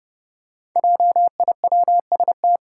Morse code, heard as a keyed tone, about 30 wpm, 700 Hz. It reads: JIWST
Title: Five Challenging Letter-Number Combinations 30wpm